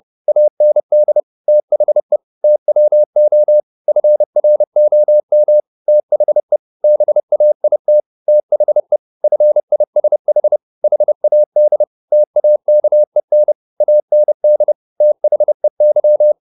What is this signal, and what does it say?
Morse code, heard as a keyed tone, about 30 wpm, 600 Hz. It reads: AND THE TWO FROM THE BAIT THE FISH HAD TAKEN AND THEY
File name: the old man and the sea / the old man and the sea 1 30wpm